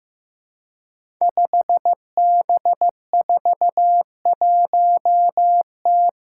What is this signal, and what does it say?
Morse code, heard as a keyed tone, about 15 wpm, 700 Hz. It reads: 5B41T
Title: Five Challenging Letter-Number Combinations 15wpm